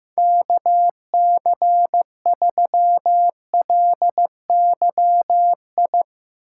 Morse code, heard as a keyed tone, about 15 wpm, 700 Hz. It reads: KC3LYI
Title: Speed Racing - USA 2x3 Call Signs 15wpm